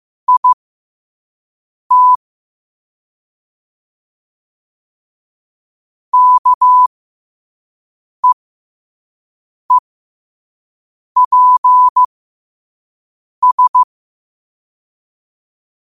Morse code, heard as a keyed tone, about 15 wpm, spaced as for 5 wpm, 1000 Hz. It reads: IT KEEPS
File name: qotdfiles / 5wpm